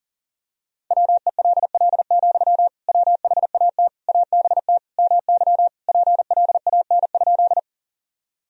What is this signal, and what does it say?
Morse code, heard as a keyed tone, about 37 wpm, 700 Hz. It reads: WELL, WHAT ABT MY PLAN?
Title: Sentences from Top 500 Words and Abbreviations 40wpm